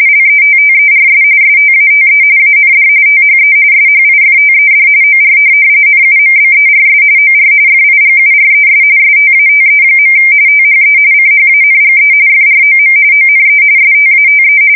RTTY, about 45.45 baud. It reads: LETTERS STATE AND THE SHIFTED OR NUMBERS OR FIGURES STATE. THE CHANGE FROM ONE STATE